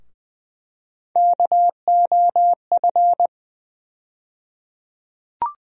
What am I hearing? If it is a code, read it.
Morse code, heard as a keyed tone, about 20 wpm, 700 Hz. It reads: KOF
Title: USA Call Sign Postfixes - 3 Letters 20wpm